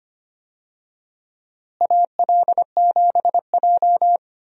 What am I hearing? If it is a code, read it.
Morse code, heard as a keyed tone, about 25 wpm, 700 Hz. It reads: AL7J
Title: USA Call Signs - 4 Characters - Repeated 2x - 25wpm